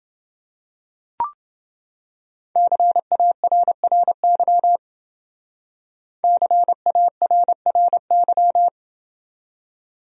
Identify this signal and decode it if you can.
Morse code, heard as a keyed tone, about 29 wpm, 700 Hz. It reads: CARRY CARRY